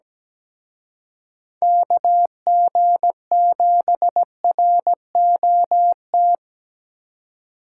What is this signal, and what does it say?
Morse code, heard as a keyed tone, about 20 wpm, 700 Hz. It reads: KG7ROT
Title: Speed Racing - USA 2x3 Call Signs 17wpm